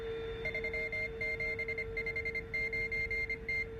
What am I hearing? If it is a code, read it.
Morse code, heard as a keyed tone, about 25 wpm, 2100 Hz. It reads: 3759T